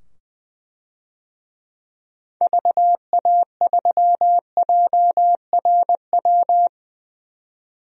Morse code, heard as a keyed tone, about 20 wpm, 700 Hz. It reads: VA3JRW
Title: DXCC-100 Call Signs - 6 Characters - 20wpm